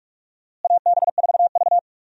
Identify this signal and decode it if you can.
Morse code, heard as a keyed tone, about 45 wpm, 700 Hz. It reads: AB4V
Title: USA Call Signs - 4 Characters - Repeated 2x - 45wpm